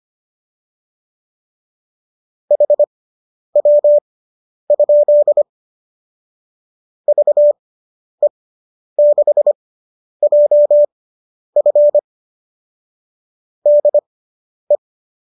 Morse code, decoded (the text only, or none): HW? VE6JF DE